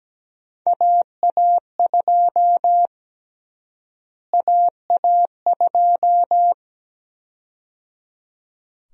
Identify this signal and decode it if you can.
Morse code, heard as a keyed tone, about 17 wpm, 700 Hz. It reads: AA2 AA2
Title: USA Call Sign Prefixes - Repeated 2x - 17wpm